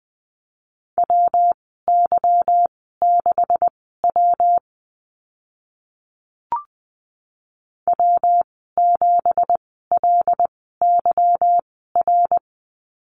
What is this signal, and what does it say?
Morse code, heard as a keyed tone, about 20 wpm, spaced as for 15 wpm, 700 Hz. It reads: WY6W W7LYR
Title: 167 Lesson 27g - Number 7 - Call Signs 20wpm